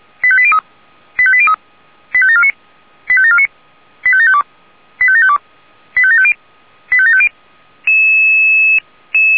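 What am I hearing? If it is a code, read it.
Five-tone selective call (ZVEI), heard as quick runs of five tones, tones of about 70 ms. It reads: 85803 85803 85749 85749 85752 85752 85750 85750
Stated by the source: ZVEI five-tone selective call 85725, received on an SDR.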